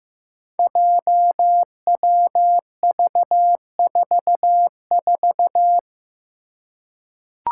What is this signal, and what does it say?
Morse code, heard as a keyed tone, about 15 wpm, 700 Hz. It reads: JWV44
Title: Five Challenging Letter-Number Combinations 15wpm